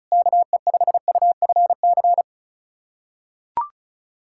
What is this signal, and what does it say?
Morse code, heard as a keyed tone, about 36 wpm, 700 Hz. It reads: KE5UFC
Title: USA Callsign DE Callsign - 6 Characters - 35wpm